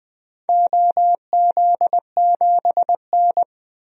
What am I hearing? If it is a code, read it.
Morse code, heard as a keyed tone, about 20 wpm, 700 Hz. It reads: OZ7N
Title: DXCC-100 Call Signs - 4 Characters - 20wpm